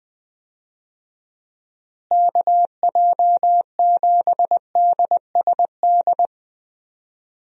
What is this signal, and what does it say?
Morse code, heard as a keyed tone, about 20 wpm, 700 Hz. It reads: KJ7DSD
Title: USA Call Signs - 6 Characters - Repeated 3x - 20wpm